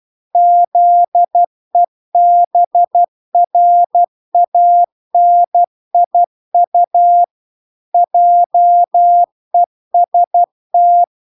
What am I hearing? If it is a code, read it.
Morse code, heard as a keyed tone, about 12 wpm, 700 Hz. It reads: ZEBRANIU JEST